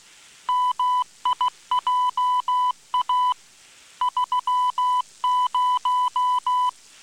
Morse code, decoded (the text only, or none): MIJA 30